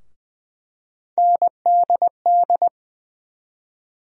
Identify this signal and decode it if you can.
Morse code, heard as a keyed tone, about 20 wpm, 700 Hz. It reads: NDD
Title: USA Call Sign Postfixes - 3 Letters 20wpm